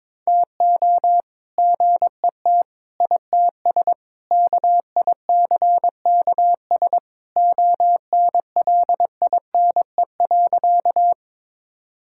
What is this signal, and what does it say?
Morse code, heard as a keyed tone, about 22 wpm, 700 Hz. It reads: TO GET ITS KICKS ONLINE.
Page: morse-code-ninja-practice-22wpm